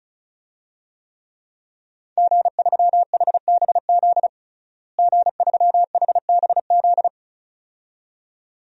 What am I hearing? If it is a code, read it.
Morse code, heard as a keyed tone, about 35 wpm, 700 Hz. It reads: G3HBZ G3HBZ
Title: DXCC-100 Call Signs - 5 Characters - Repeated 2x - 35wpm